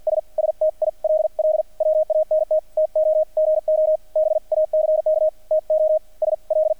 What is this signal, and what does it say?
Morse code, heard as a keyed tone, about 50 wpm, spaced as for 43 wpm, 620 Hz. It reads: S STILL WANT TO GO BACK TO SL
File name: archived / CRSnet